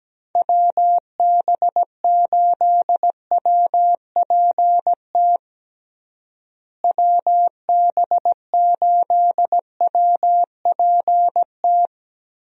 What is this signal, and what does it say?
Morse code, heard as a keyed tone, about 17 wpm, 700 Hz. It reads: WB8WPT WB8WPT
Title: USA Call Signs - 6 Characters - Repeated 3x - 17wpm